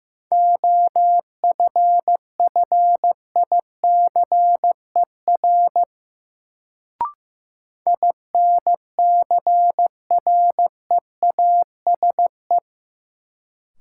Morse code, heard as a keyed tone, about 15 wpm, 700 Hz. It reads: OFFICER INCREASE
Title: Sentences from Top 1200 Words - New Word Drill Alt - 15wpm